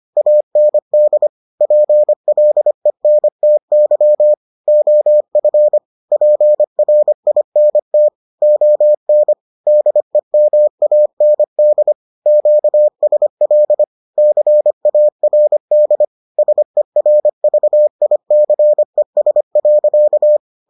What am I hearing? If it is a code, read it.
Morse code, heard as a keyed tone, about 25 wpm, 600 Hz. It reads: AND PLENTY OF PRINT ON DEMAND QSL CARD SERVICES.